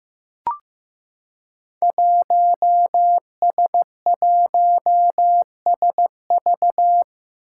Morse code, heard as a keyed tone, about 15 wpm, 700 Hz. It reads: 1S1SV
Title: Five Challenging Letter-Number Combinations 15wpm